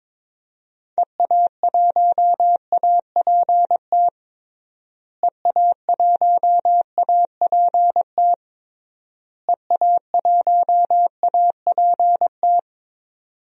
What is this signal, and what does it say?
Morse code, heard as a keyed tone, about 22 wpm, 700 Hz. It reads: EA1APT EA1APT EA1APT
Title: DXCC-100 Call Signs - 6 Characters - Repeated 3x - 22wpm